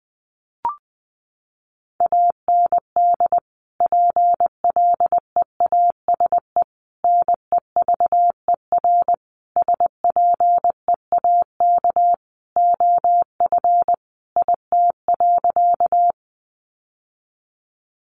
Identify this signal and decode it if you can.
Morse code, heard as a keyed tone, about 20 wpm, 700 Hz. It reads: AND PLEASE NEVER SPEAK OF IT.
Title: Sentences from Top 1500 Words 20wpm